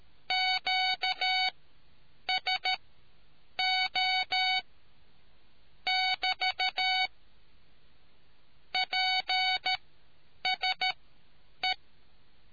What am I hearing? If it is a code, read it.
Morse code, heard as a keyed tone, about 13 wpm, spaced as for 8 wpm, 750 Hz. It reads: QSO <BT> PSE